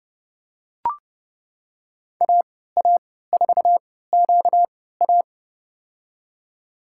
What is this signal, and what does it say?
Morse code, heard as a keyed tone, about 30 wpm, spaced as for 17 wpm, 700 Hz. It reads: AA4QA
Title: USA Call Signs - 5 Characters - 3x Character Spacing - 30wpm